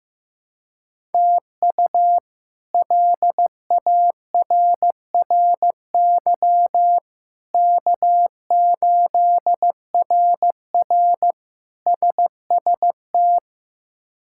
Morse code, heard as a keyed tone, about 15 wpm, 700 Hz. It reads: TU LARRY K8RR SST
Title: K1USN SST QSO Practice - 15wpm